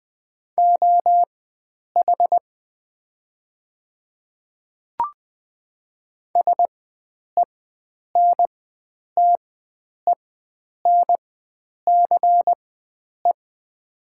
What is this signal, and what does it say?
Morse code, heard as a keyed tone, about 21 wpm, spaced as for 9 wpm, 700 Hz. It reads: OH SENTENCE
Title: 4x Character Spacing - Top 500 Words - 20wpm